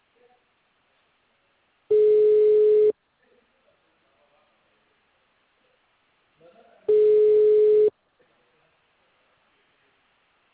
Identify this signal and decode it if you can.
Ringback tone (Europe, 425 Hz).